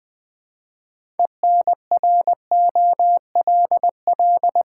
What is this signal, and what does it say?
Morse code, heard as a keyed tone, about 20 wpm, 700 Hz. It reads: ENROLL
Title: Build A Word - 3 Letters At A Time - 6-Letter Words - 20wpm